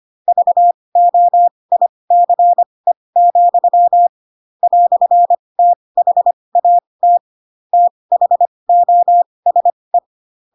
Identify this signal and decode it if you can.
Morse code, heard as a keyed tone, about 25 wpm, spaced as for 20 wpm, 700 Hz. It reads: VOICE, "THAT THOSE
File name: Pinocchio / Pinocchio ch4 15 25 ik1hge